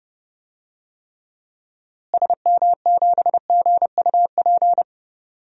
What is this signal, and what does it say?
Morse code, heard as a keyed tone, about 30 wpm, 700 Hz. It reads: SM7GUP
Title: DXCC-100 Call Signs - 6 Characters - Repeated 2x - 30wpm